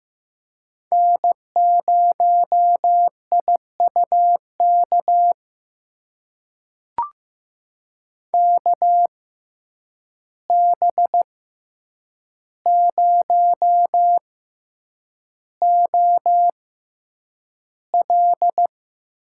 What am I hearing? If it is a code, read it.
Morse code, heard as a keyed tone, about 15 wpm, spaced as for 8 wpm, 700 Hz. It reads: N0IUK KB0OL
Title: USA Call Signs - 6x Character Spacing - 15wpm